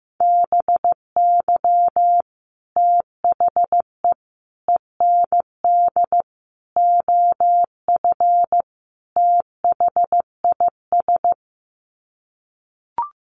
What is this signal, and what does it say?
Morse code, heard as a keyed tone, about 15 wpm, 700 Hz. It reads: BY THE END OF THIS
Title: Sets of 5 Words Encore 15wpm